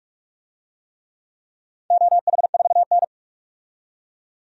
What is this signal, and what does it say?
Morse code, heard as a keyed tone, about 46 wpm, 700 Hz. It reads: OH4N
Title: DXCC-100 Call Signs - 4 Characters - 45wpm